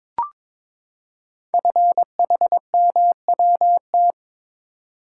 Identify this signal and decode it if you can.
Morse code, heard as a keyed tone, about 22 wpm, 700 Hz. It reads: FHMWT
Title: Code Groups - 5 Letters - 22wpm